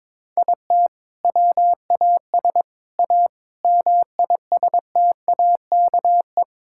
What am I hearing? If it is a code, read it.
Morse code, heard as a keyed tone, about 21 wpm, 700 Hz. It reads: IT WAS A MISTAKE
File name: practice / 22wpm / Sentences with 4-Word Phrases - 22wpm